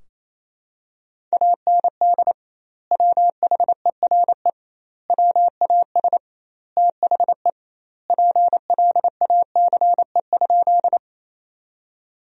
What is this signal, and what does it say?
Morse code, heard as a keyed tone, about 28 wpm, 700 Hz. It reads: AND WHERE WAS THE PLACE?
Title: 2x Word-Spacing - Sentences from Top 200 Words - 28wpm